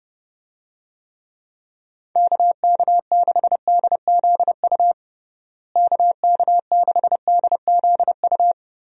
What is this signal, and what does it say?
Morse code, heard as a keyed tone, about 30 wpm, 700 Hz. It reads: KK6DZU KK6DZU
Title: USA Call Signs - 6 Characters - Repeated 3x - 30wpm